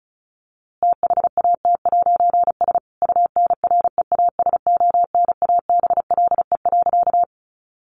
Morse code, heard as a keyed tone, about 34 wpm, 700 Hz. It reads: THAT'S UNREASONABLE.
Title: A Sound Of Thunder 35wpm